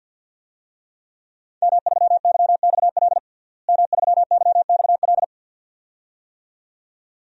Morse code, heard as a keyed tone, about 47 wpm, 700 Hz. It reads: M3YXL M3YXL
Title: DXCC-100 Call Signs - 5 Characters - Repeated 2x - 50wpm